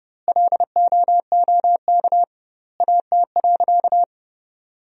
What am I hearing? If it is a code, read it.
Morse code, heard as a keyed tone, about 30 wpm, 700 Hz. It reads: LOOK AT.
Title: 2x Word-Spacing - Sentences from Top 200 Words - 30wpm